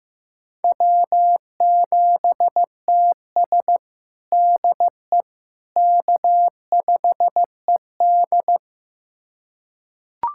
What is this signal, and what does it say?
Morse code, heard as a keyed tone, about 15 wpm, 700 Hz. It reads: W7TS DE K5ED
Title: USA Callsign DE Callsign - 4 Characters - 15wpm